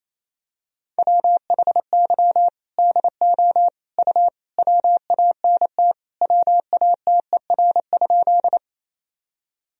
Morse code, heard as a keyed tone, about 28 wpm, 700 Hz. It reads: WHY DO U WANT WATER?
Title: Sentences from Top 200 Words and Abbreviations 28wpm